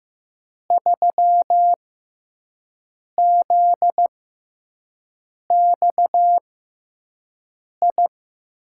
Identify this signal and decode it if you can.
Morse code, heard as a keyed tone, about 15 wpm, spaced as for 5 wpm, 700 Hz. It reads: 3ZXI
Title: USA Call Signs - 5 Characters - 6x Character Spacing - 15wpm